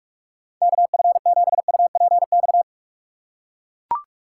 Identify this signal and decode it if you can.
Morse code, heard as a keyed tone, about 41 wpm, 700 Hz. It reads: KF7UPX
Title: USA Call Signs - 3x Character Spacing - 45wpm